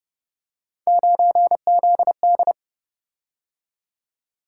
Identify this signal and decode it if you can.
Morse code, heard as a keyed tone, about 30 wpm, 700 Hz. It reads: 9ZD
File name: Speed-Racing - Three Letter-Number Combinations - 25wpm